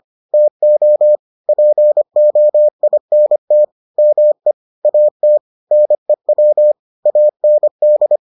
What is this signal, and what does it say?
Morse code, heard as a keyed tone, about 25 wpm, 600 Hz. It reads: TO POINT ME AT NEW AND